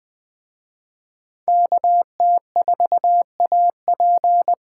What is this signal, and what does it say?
Morse code, heard as a keyed tone, about 20 wpm, 700 Hz. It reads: KT4AP